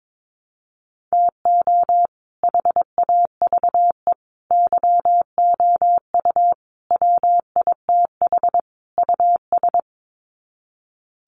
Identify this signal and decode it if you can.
Morse code, heard as a keyed tone, about 22 wpm, 700 Hz. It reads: TO HAVE YOU WITH US
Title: Sets of 5 Words 22wpm